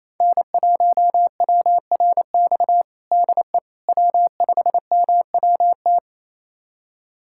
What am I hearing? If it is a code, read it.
Morse code, heard as a keyed tone, about 28 wpm, 700 Hz. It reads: N1WRX DE W5MWT
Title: USA Callsign DE Callsign - 5 Characters - 28wpm